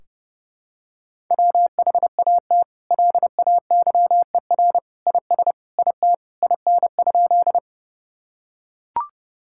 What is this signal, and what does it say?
Morse code, heard as a keyed tone, about 30 wpm, 700 Hz. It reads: WHAT LAYER IS IT IN?
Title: Sentences from Top 1600 Words 30wpm